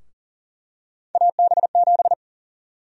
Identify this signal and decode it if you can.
Morse code, heard as a keyed tone, about 40 wpm, 700 Hz. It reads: AB7
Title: USA Call Sign Prefixes - 40wpm